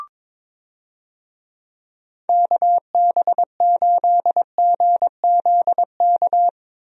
Morse code, heard as a keyed tone, about 22 wpm, 700 Hz. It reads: KB8GZK
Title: USA Call Signs - 6 Characters - Repeated 2x - 22wpm